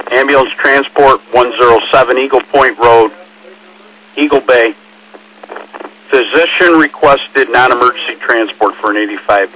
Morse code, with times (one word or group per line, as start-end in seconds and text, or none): none